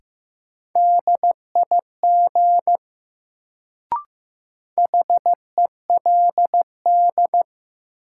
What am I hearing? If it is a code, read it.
Morse code, heard as a keyed tone, about 15 wpm, 700 Hz. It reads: DIG HELD
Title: Sentences from Top 1600 Words - New Word Drill - 15wpm